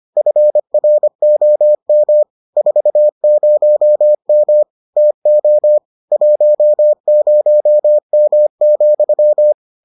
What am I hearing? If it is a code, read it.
Morse code, heard as a keyed tone, about 25 wpm, 600 Hz. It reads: FROM 40M TO 10M,